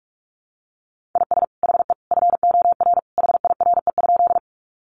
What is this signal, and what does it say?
Morse code, heard as a keyed tone, about 44 wpm, 700 Hz. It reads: IS HE FOR HIRE?
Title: Sentences from Top 1400 Words 45wpm